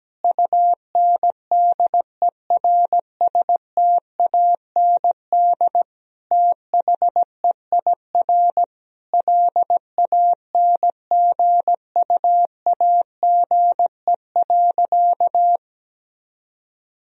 Morse code, heard as a keyed tone, about 17 wpm, 700 Hz. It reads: UNDERSTAND THEIR LANGUAGE.
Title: Sentences from Top 1000 Words - Review Part 3 - 17wpm